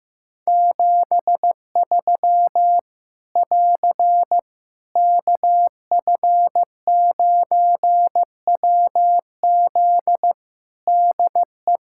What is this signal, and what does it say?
Morse code, heard as a keyed tone, about 15 wpm, 700 Hz. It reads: 73 <AR> KF9WZ DE
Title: K1USN SST QSO Practice 15wpm